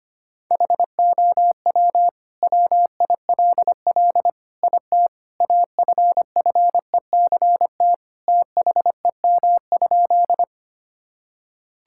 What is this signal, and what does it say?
Morse code, heard as a keyed tone, about 25 wpm, 700 Hz. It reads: HOW WILL IT AFFECT THEM?
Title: Sentences from Top 1000 Words - Review Part 6 - 25wpm